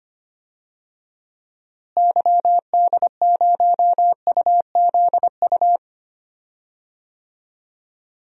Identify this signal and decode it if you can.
Morse code, heard as a keyed tone, about 25 wpm, 700 Hz. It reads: YD0UZU
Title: DXCC-100 Call Signs - 6 Characters - Repeated 3x - 25wpm